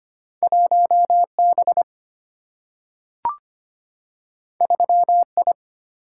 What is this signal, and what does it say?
Morse code, heard as a keyed tone, about 25 wpm, 700 Hz. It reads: 1B 3I
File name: Two Letter-Number Combinations - 25wpm